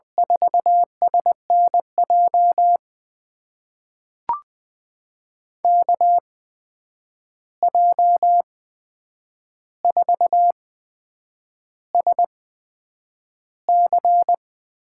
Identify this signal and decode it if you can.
Morse code, heard as a keyed tone, about 20 wpm, spaced as for 8 wpm, 700 Hz. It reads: 4SNJ KJ4SC
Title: USA Call Signs - 6 Characters - 8x Character Spacing - 20wpm